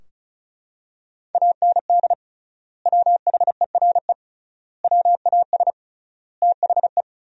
Morse code, heard as a keyed tone, about 34 wpm, 700 Hz. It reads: AND WHERE WAS THE
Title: 3x Word-Spacing - Sentences from Top 200 Words - 35wpm